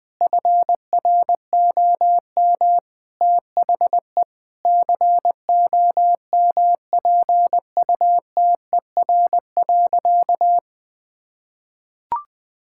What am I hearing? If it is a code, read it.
Morse code, heard as a keyed tone, about 20 wpm, 700 Hz. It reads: FROM THE COMPUTER.
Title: Sentences from Top 1000 Words - Review Part 6 - 20wpm